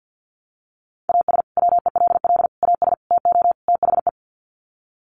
Morse code, heard as a keyed tone, about 50 wpm, 700 Hz. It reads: AS WELL AS TO THE
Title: Sets of 5 Words Encore2 50wpm